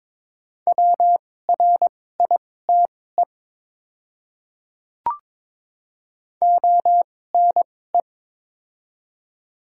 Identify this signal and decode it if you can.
Morse code, heard as a keyed tone, about 22 wpm, spaced as for 16 wpm, 700 Hz. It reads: WRITE ONE
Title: 2x Character Spacing - Top 100 Words - 22wpm